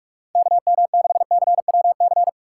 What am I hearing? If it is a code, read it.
Morse code, heard as a keyed tone, about 46 wpm, 690 Hz. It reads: KM6CWC
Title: USA Call Signs - 6 Characters - 3x Character Spacing - 45wpm